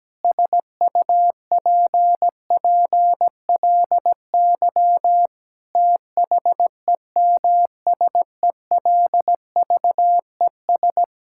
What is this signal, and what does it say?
Morse code, heard as a keyed tone, about 17 wpm, 700 Hz. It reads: SUPPLY THEMSELVES
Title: Sentences from Top 1000 Words - Review Part 5 - 17wpm